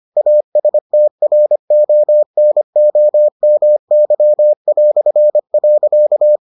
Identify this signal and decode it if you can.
Morse code, heard as a keyed tone, about 25 wpm, 600 Hz. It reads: ASTRONOMY".